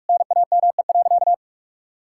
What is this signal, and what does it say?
Morse code, heard as a keyed tone, about 46 wpm, 700 Hz. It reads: NAME.
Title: Sentences from Top 1000 Words - Review Part 5 - 45wpm